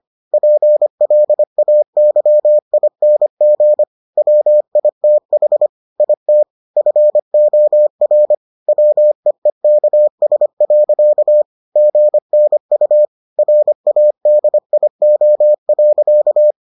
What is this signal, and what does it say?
Morse code, heard as a keyed tone, about 25 wpm, 600 Hz. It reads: PLAYING WITH IT FOR WEEKS. GNU RADIO.